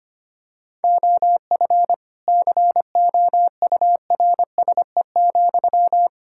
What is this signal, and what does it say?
Morse code, heard as a keyed tone, about 25 wpm, 700 Hz. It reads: OF COURSE,
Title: Sentences from Top 1600 Words 25wpm